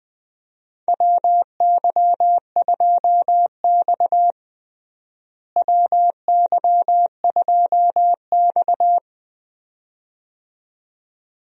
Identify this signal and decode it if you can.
Morse code, heard as a keyed tone, about 20 wpm, 700 Hz. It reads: WY2X WY2X
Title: USA Call Signs - 4 Characters - Repeated 2x - 20wpm